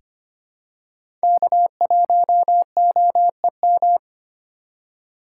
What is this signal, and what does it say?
Morse code, heard as a keyed tone, about 25 wpm, 700 Hz. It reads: K1OEM
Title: USA Call Signs - 5 Characters - 4x Character Spacing - 25wpm